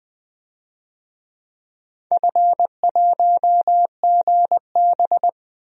Morse code, heard as a keyed tone, about 20 wpm, 700 Hz. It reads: F1GB
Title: DXCC-100 Call Signs - 4 Characters - 20wpm